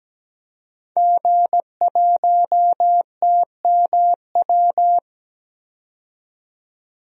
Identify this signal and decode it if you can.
Morse code, heard as a keyed tone, about 17 wpm, 700 Hz. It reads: G1TMW
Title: DXCC-100 Call Signs - 5 Characters - 17wpm